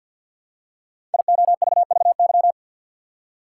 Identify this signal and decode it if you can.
Morse code, heard as a keyed tone, about 53 wpm, 700 Hz. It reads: IOVVY
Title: Code Groups - 5 Letter:Numbers - 50wpm